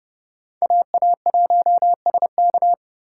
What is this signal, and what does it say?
Morse code, heard as a keyed tone, about 37 wpm, 700 Hz. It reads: AA1SK
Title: Speed Racing - USA Call Signs 30wpm